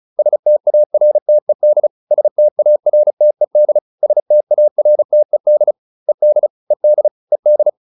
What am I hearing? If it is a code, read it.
Morse code, heard as a keyed tone, about 34 wpm, 600 Hz. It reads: STARTED STARTED STARTED ED ED ED